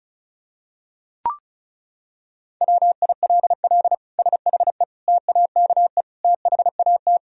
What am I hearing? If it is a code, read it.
Morse code, heard as a keyed tone, about 36 wpm, 700 Hz. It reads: WILL SHE TAKE THAT
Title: Sentences from Top 1000 Words - Review Part 4 - 35wpm